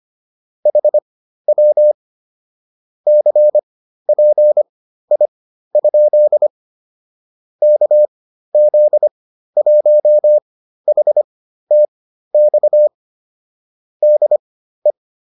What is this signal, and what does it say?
Morse code, heard as a keyed tone, about 25 wpm, spaced as for 13 wpm, 600 Hz. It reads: HW CPI? KZ1HTX DE
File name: QSO301 13